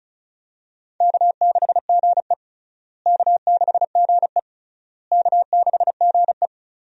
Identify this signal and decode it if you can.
Morse code, heard as a keyed tone, about 36 wpm, 700 Hz. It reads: K6GE K6GE K6GE
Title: USA Call Signs - 4 Characters - Repeated 3x - 35wpm